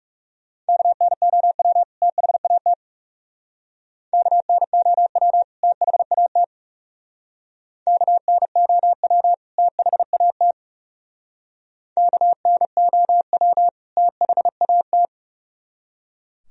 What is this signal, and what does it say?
Morse code, heard as a keyed tone, about 36 wpm, 700 Hz. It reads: KNOW THAT KNOW THAT KNOW THAT KNOW THAT
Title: Speed-Racing - Sets of 2 Words 30wpm